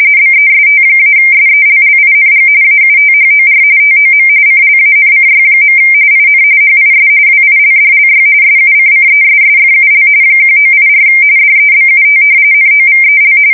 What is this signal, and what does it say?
RTTY, about 45.45 baud. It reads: BEST-SELLING AUTHOR TOM CLANCY. IN HIS WILDLY SUCCESSFUL 1984 DEBUT NOVEL
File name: fsk170 / battleshipnj